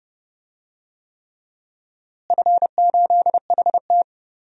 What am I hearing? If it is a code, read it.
Morse code, heard as a keyed tone, about 31 wpm, 700 Hz. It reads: F8HT
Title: DXCC-100 Call Signs - 4 Characters - Repeated 3x - 30wpm